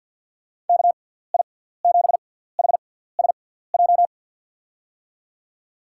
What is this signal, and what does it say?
Morse code, heard as a keyed tone, about 50 wpm, spaced as for 17 wpm, 700 Hz. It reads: KI7HSJ
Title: USA Call Signs - 6 Characters - 6x Character Spacing - 50wpm